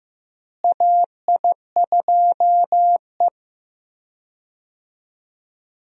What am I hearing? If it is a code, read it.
Morse code, heard as a keyed tone, about 15 wpm, 700 Hz. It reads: AI2E
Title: USA Call Signs - 4 Characters - Repeated 3x - 15wpm